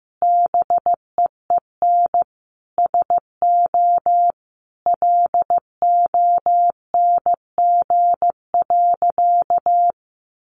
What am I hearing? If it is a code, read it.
Morse code, heard as a keyed tone, about 15 wpm, 700 Hz. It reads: BEEN SO LONG.
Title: Sentences from Top 100 Words 15wpm